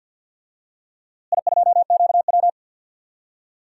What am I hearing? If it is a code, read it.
Morse code, heard as a keyed tone, about 47 wpm, 700 Hz. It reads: I2QW
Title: DXCC-100 Call Signs - 4 Characters - Repeated 3x - 50wpm